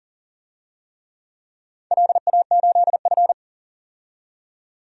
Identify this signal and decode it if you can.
Morse code, heard as a keyed tone, about 40 wpm, 700 Hz. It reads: LA8F
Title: DXCC-100 Call Signs - 4 Characters - 40wpm